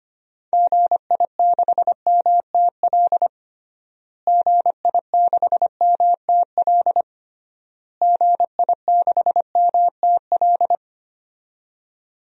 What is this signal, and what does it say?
Morse code, heard as a keyed tone, about 25 wpm, 700 Hz. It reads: GI6MTL GI6MTL GI6MTL
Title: DXCC-100 Call Signs - 6 Characters - Repeated 3x - 25wpm